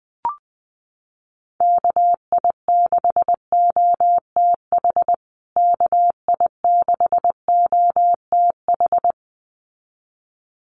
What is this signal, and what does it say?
Morse code, heard as a keyed tone, about 20 wpm, 700 Hz. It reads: KI6OTH KI6OTH
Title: USA Call Signs Repeated 2x 20wpm